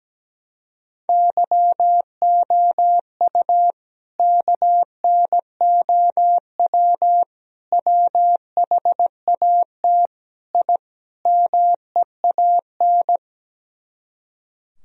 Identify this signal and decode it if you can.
Morse code, heard as a keyed tone, about 17 wpm, 700 Hz. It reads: YOU KNOW WHAT I MEAN
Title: Speed-Racing - Sets of 5 Words 17wpm